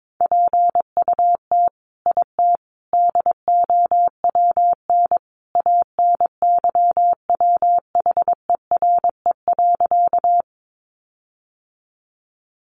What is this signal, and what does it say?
Morse code, heard as a keyed tone, about 22 wpm, 700 Hz. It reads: PUT IT DOWN ANYWHERE.
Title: Sentences from Top 1400 Words 22wpm